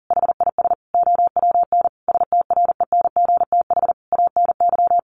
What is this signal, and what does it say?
Morse code, heard as a keyed tone, about 37 wpm, 700 Hz. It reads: HIS OWN STRENGTH ANY
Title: Sentences from Top 1300 Words 40wpm